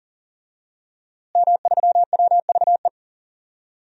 Morse code, heard as a keyed tone, about 40 wpm, 700 Hz. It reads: M3WVE
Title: DXCC-100 Call Signs - 5 Characters - Repeated 3x - 40wpm